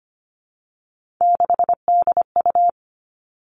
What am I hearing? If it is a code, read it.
Morse code, heard as a keyed tone, about 25 wpm, 700 Hz. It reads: 6DU
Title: Three Challenging Letter-Number Combinations 25wpm